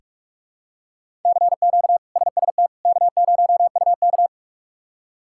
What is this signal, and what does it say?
Morse code, heard as a keyed tone, about 42 wpm, 700 Hz. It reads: CQ SST K0UK
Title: K1USN SST QSO Practice - 45wpm